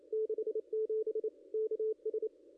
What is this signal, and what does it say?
Morse code, heard as a keyed tone, about 27 wpm, 440 Hz. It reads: 67 KS
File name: contests / ss-cw-k5tr-op-n5rz -0200